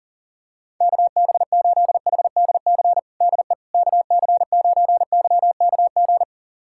Morse code, heard as a keyed tone, about 40 wpm, 700 Hz. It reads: KB8HDC DE KC9YKG
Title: USA Callsign DE Callsign - 6 Characters - 40wpm